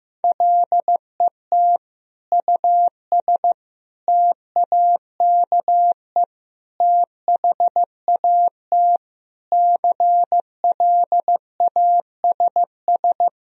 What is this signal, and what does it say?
Morse code, heard as a keyed tone, about 15 wpm, 700 Hz. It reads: LET US TAKE THAT CLASS
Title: Sentences from Top 1000 Words - Review Part 2 - 15wpm